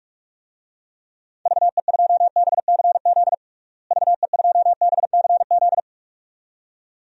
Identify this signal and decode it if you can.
Morse code, heard as a keyed tone, about 46 wpm, 700 Hz. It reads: VE2BCZ VE2BCZ
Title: DXCC-100 Call Signs - 6 Characters - Repeated 2x - 45wpm